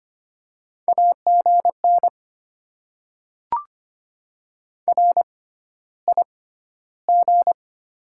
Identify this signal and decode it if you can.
Morse code, heard as a keyed tone, about 25 wpm, spaced as for 13 wpm, 700 Hz. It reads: AGN RIG
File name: Top 90 QSO Elements - 6x Character Spacing - 25wpm